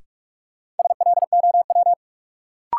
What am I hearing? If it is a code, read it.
Morse code, heard as a keyed tone, about 46 wpm, 700 Hz. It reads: SLOW
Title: Sentences from Top 400 Words - New Word Drill - 45wpm